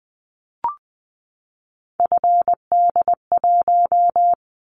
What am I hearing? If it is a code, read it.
Morse code, heard as a keyed tone, about 20 wpm, 700 Hz. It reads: FD1
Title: Callsign Prefixes for Top 25 DXCC - 20wpm